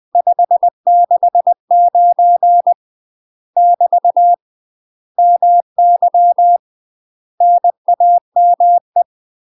Morse code, heard as a keyed tone, about 20 wpm, 700 Hz. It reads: 569 <BT> MY NAME
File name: UK MASTER QSOs TRAINING / 20WPM / UK MASTER QSOs 0071